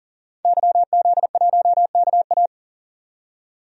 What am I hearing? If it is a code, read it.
Morse code, heard as a keyed tone, about 40 wpm, 700 Hz. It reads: YZ1KA
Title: DXCC-100 Call Signs - 5 Characters - 40wpm